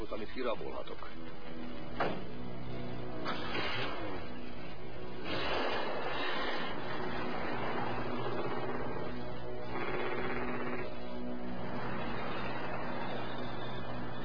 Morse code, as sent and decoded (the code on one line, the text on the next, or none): none